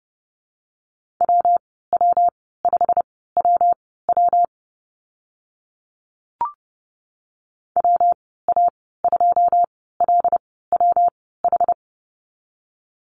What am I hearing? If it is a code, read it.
Morse code, heard as a keyed tone, about 30 wpm, spaced as for 17 wpm, 700 Hz. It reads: WW5WW WA2LWH
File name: Lesson 19g - Letter w - Call Signs 30wpm